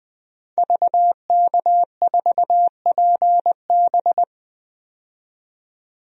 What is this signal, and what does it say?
Morse code, heard as a keyed tone, about 21 wpm, 700 Hz. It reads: VK4PB
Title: DXCC-100 Call Signs - 5 Characters - 20wpm